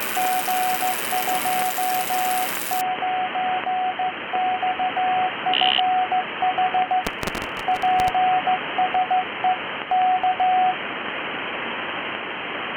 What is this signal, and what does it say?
Morse code, heard as a keyed tone, about 15 wpm, 740 Hz, with radio noise. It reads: G29XFH PSE K